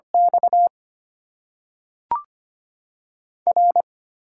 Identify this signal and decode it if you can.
Morse code, heard as a keyed tone, about 25 wpm, 690 Hz. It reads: X R
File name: Top 90 QSO Elements - 8x Character Spacing - 25wpm